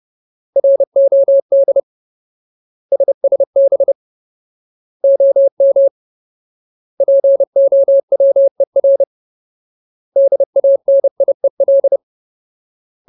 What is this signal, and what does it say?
Morse code, heard as a keyed tone, about 30 wpm, 550 Hz. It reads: ROD SSB OM POWER DANIEL
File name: Ultimate Word List / 30WPM Ultimate